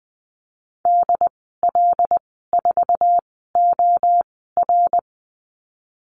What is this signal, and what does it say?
Morse code, heard as a keyed tone, about 20 wpm, spaced as for 15 wpm, 700 Hz. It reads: DL4OR
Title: 088 Lesson 15g - Number 5 - Call Signs 20wpm